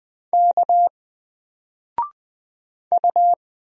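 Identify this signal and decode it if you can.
Morse code, heard as a keyed tone, about 20 wpm, 700 Hz. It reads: K U
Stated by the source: Single letter and number Morse drill at 20 wpm, from N4LCD.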